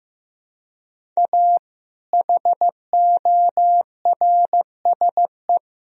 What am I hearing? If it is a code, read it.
Morse code, heard as a keyed tone, about 16 wpm, 700 Hz. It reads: A HORSE
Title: Sentences from Top 1000 Words - Review Part 1 - 15wpm